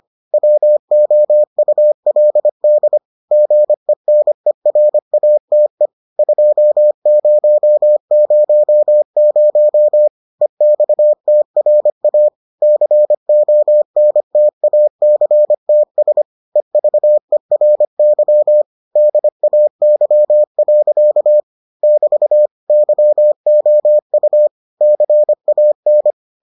Morse code, heard as a keyed tone, about 25 wpm, 600 Hz. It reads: WOULD GENERATE 2000 EXTRA CONTACTS EVERY DAY. <BT> YOU CAN